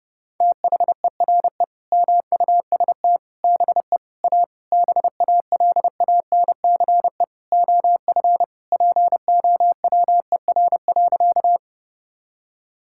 Morse code, heard as a keyed tone, about 30 wpm, 700 Hz. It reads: THERE MUST BE A BALANCE OF POWER.
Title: Sentences from Top 1000 Words - Review Part 6 - 30wpm